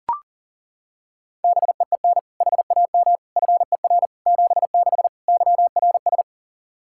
Morse code, heard as a keyed tone, about 40 wpm, 700 Hz. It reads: BEEN HAM FER 76 YRS
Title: K1USN SST QSO Practice 40wpm